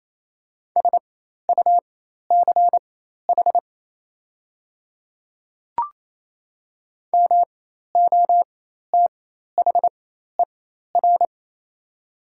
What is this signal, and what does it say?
Morse code, heard as a keyed tone, about 28 wpm, spaced as for 13 wpm, 700 Hz. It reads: SUCH MOTHER
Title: 4x Character Spacing - Top 500 Words - 28wpm